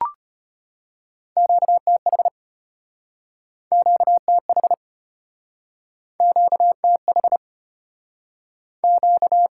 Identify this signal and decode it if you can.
Morse code, heard as a keyed tone, about 34 wpm, 700 Hz. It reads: QTH QTH QTH Q